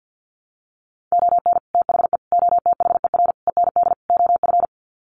Morse code, heard as a keyed tone, about 50 wpm, 700 Hz. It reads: ON THE OTHER END OF
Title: Sets of 5 Words Encore 50wpm